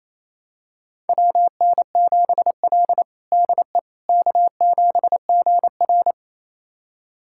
Morse code, heard as a keyed tone, about 28 wpm, 700 Hz. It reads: WN7L DE K7GR
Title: USA Callsign DE Callsign - 4 Characters - 28wpm